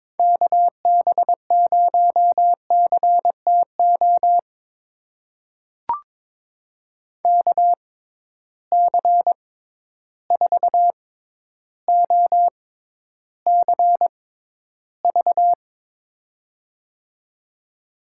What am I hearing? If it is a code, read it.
Morse code, heard as a keyed tone, about 22 wpm, spaced as for 11 wpm, 700 Hz. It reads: KB0CTO KC4OCV
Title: USA Call Signs - 6 Characters - 6x Character Spacing - 22wpm